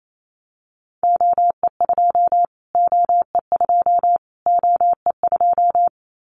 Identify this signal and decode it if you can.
Morse code, heard as a keyed tone, about 28 wpm, 700 Hz. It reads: OE2 OE2 OE2
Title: Callsign Prefixes for Top 25 DXCC - Repeated 3x - 28wpm